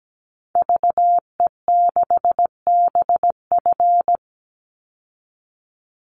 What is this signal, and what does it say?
Morse code, heard as a keyed tone, about 17 wpm, 700 Hz. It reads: VE6BF
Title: Call Signs Repeated 2x - 17wpm